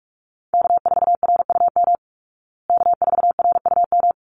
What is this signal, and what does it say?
Morse code, heard as a keyed tone, about 44 wpm, 700 Hz. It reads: K4RUM K4RUM
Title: Call Signs Repeated 3x - 45wpm